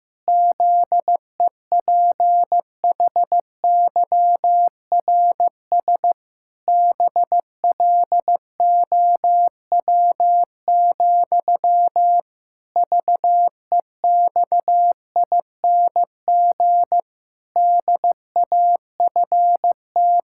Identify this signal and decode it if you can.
Morse code, heard as a keyed tone, about 16 wpm, 700 Hz. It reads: ZEPHYRS BLOW, VEXING DAFT
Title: Pangram Challenge v2 - 15wpm